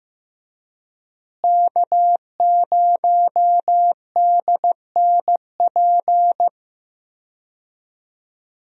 Morse code, heard as a keyed tone, about 15 wpm, 700 Hz. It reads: K0DNP
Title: USA Call Signs - 5 Characters - Repeated 2x - 15wpm